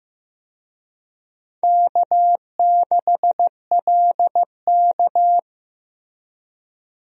Morse code, heard as a keyed tone, about 15 wpm, 700 Hz. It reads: K6LK
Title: USA Call Signs - 4 Characters - Repeated 3x - 15wpm